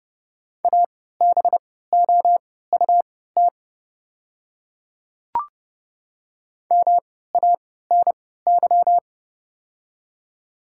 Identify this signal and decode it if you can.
Morse code, heard as a keyed tone, about 31 wpm, spaced as for 17 wpm, 700 Hz. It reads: ABOUT MANY
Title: 3x Character Spacing - Top 100 Words - 30wpm